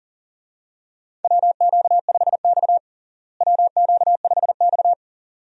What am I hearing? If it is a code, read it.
Morse code, heard as a keyed tone, about 40 wpm, 700 Hz. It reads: WQ5X WQ5X
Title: USA Call Signs - 4 Characters - Repeated 2x - 40wpm